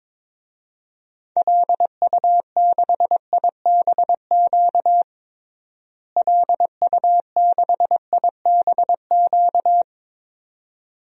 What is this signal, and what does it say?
Morse code, heard as a keyed tone, about 22 wpm, 700 Hz. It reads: LU6IBQ LU6IBQ
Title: DXCC-100 Call Signs - 6 Characters - Repeated 2x - 22wpm